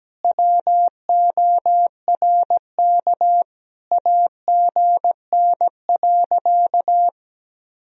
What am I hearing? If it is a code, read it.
Morse code, heard as a keyed tone, about 17 wpm, 700 Hz. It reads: WORK AGN.
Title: Sentences from Top 200 Words and Abbreviations 17wpm